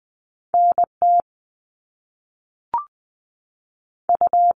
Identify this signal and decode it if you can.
Morse code, heard as a keyed tone, about 20 wpm, 700 Hz. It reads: NT U
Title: Top Two Letter Combinations in Words 20wpm